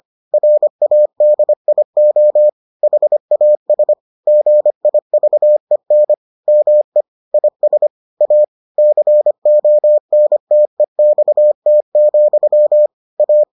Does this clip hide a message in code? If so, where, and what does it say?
Yes: Morse code from 0.3 to 13.4 s, heard as a keyed tone, about 25 wpm, 600 Hz. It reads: RADIO HAS GIVEN ME IS A CONTEXT, A